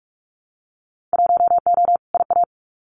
Morse code, heard as a keyed tone, about 44 wpm, 700 Hz. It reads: 1O IA
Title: Winter Field Day Exchanges 45wpm